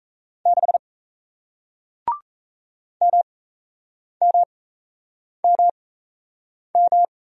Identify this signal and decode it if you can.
Morse code, heard as a keyed tone, about 37 wpm, spaced as for 8 wpm, 700 Hz. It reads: 6 MMMM